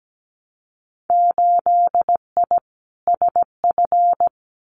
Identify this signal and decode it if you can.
Morse code, heard as a keyed tone, about 17 wpm, 700 Hz. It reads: 8I SF
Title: Winter Field Day Exchanges Encore 17wpm